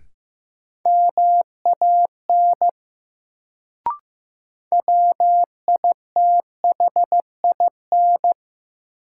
Morse code, heard as a keyed tone, about 15 wpm, 700 Hz. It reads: MAN WITHIN
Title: Sets of 2 Words - New Word Drill - 15wpm